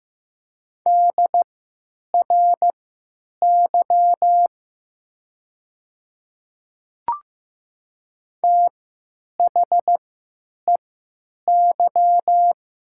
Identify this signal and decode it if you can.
Morse code, heard as a keyed tone, about 15 wpm, spaced as for 9 wpm, 700 Hz. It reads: DRY THEY
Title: 3x Character Spacing - Top 500 Words - 15wpm